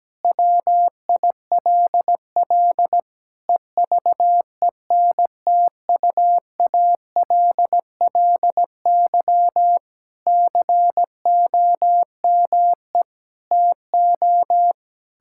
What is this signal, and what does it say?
Morse code, heard as a keyed tone, about 17 wpm, 700 Hz. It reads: WILL EVENTUALLY COME TO
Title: Sentences from Top 1000 Words - Review Part 6 - 17wpm